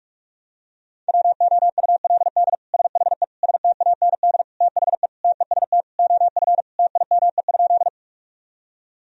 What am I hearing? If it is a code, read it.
Morse code, heard as a keyed tone, about 46 wpm, 700 Hz. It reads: WOULD SHE STAND THE TEST OF TIME?
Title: Sentences from Top 1000 Words - Review Part 3 - 45wpm